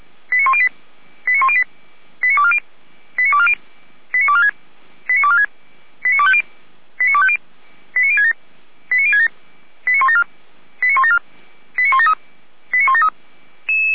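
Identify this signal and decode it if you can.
Five-tone selective call (ZVEI), heard as quick runs of five tones, tones of about 70 ms. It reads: 89108 89108 89240 89240 89246 89246 89250 89250 89067 89067 89174 89174 89173 89173
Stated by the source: Five-tone ZVEI selective call received on SDR, from Feuerwehr Orlamünde fire brigade audio server.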